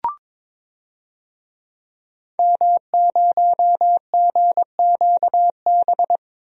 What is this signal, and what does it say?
Morse code, heard as a keyed tone, about 22 wpm, 700 Hz. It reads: M0GQB
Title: DXCC-100 Call Signs - 5 Characters - Repeated 3x - 22wpm